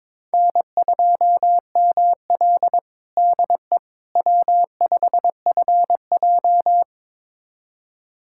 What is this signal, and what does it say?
Morse code, heard as a keyed tone, about 21 wpm, 700 Hz. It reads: N2ML DE W5FJ
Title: USA Callsign DE Callsign - 4 Characters - 22wpm